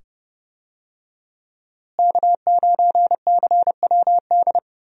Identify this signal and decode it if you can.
Morse code, heard as a keyed tone, about 29 wpm, 700 Hz. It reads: K9CWD
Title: USA Call Signs - 5 Characters - Repeated 2x - 30wpm